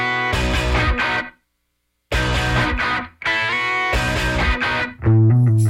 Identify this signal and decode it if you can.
Emergency Alert System (SAME) tones, heard as harsh buzzing bursts. It is heard from 1.0 to 4.9 s.
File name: wrct-log hi- 19